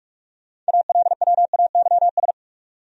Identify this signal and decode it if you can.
Morse code, heard as a keyed tone, about 58 wpm, 700 Hz. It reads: ALWAYS